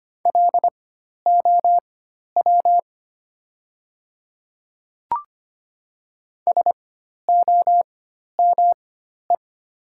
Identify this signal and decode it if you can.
Morse code, heard as a keyed tone, about 25 wpm, spaced as for 12 wpm, 700 Hz. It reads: LOW SOME
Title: 4x Character Spacing - Top 200 Words - 25wpm